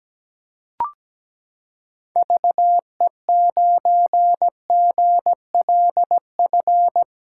Morse code, heard as a keyed tone, about 17 wpm, 700 Hz. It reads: VE9GLF
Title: K1USN SST QSO Practice - 17wpm